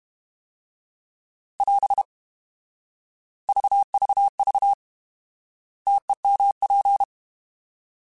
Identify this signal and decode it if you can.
Morse code, heard as a keyed tone, about 32 wpm, 800 Hz. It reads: <AS> VVV TEMP
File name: Alphabetical / Alphabetical mp3s / Ts @30